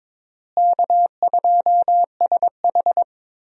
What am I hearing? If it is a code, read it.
Morse code, heard as a keyed tone, about 22 wpm, 700 Hz. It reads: K2SH
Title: USA Call Signs - 4 Characters - Repeated 3x - 22wpm